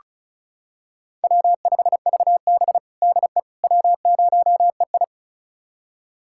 Morse code, heard as a keyed tone, about 34 wpm, 700 Hz. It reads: W5VB DE W0EI
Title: USA Callsign DE Callsign - 4 Characters - 35wpm